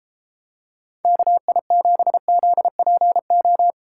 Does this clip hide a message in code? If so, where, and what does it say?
Yes: Morse code from 1.1 to 3.7 s, heard as a keyed tone, about 33 wpm, 700 Hz. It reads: KI7ZPO